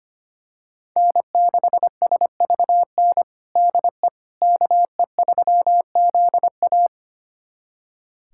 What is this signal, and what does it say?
Morse code, heard as a keyed tone, about 25 wpm, 700 Hz. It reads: N6SVN DE KE3ZA
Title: USA Callsign DE Callsign - 5 Characters - 25wpm